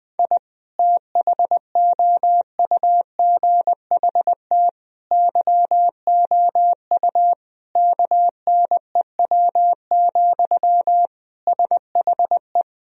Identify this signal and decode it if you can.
Morse code, heard as a keyed tone, about 20 wpm, 700 Hz. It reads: I THOUGHT YOU KNEW, SHE
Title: Sentences from Top 1000 Words - Review Part 4 - 20wpm